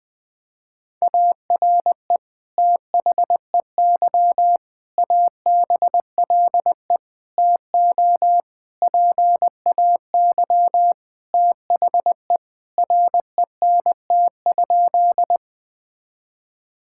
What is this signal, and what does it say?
Morse code, heard as a keyed tone, about 20 wpm, 700 Hz. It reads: ARE THEY ABLE TO PAY THE RENT?
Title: Sentences from Top 1000 Words - Review Part 5 - 20wpm